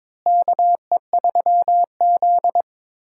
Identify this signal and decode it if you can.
Morse code, heard as a keyed tone, about 22 wpm, 700 Hz. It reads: KE3Z
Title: USA Call Signs - 4 Characters - Repeated 3x - 22wpm